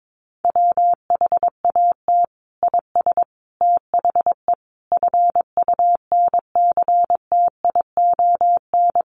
Morse code, heard as a keyed tone, about 22 wpm, 700 Hz. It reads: WHAT IS THE FUNCTION
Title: Sentences from Top 1100 Words 22wpm